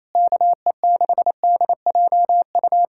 Morse code, heard as a keyed tone, about 26 wpm, 700 Hz. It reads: KE6DJU
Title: Speed-Racing Common QSO Phrases - 28wpm